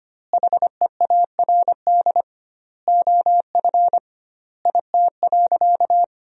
Morse code, heard as a keyed tone, about 25 wpm, 700 Hz. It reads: HEARD OF IT.
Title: 2x Word-Spacing - Sentences from Top 400 Words - 25wpm